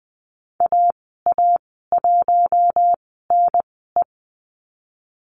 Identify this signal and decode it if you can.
Morse code, heard as a keyed tone, about 20 wpm, spaced as for 14 wpm, 700 Hz. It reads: AA1NE
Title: 042 Lesson 8g - Number 1 - Call Signs 20wpm